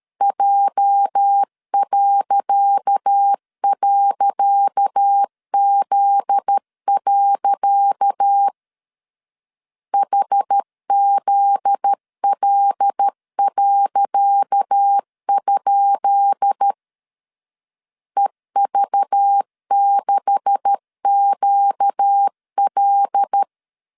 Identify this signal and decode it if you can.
Morse code, heard as a keyed tone, about 13 wpm, 800 Hz. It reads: J..Z. HZL.? EV6QL
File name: LEZIONI / 10 SIGNS / 10C DOT 12wpm 2451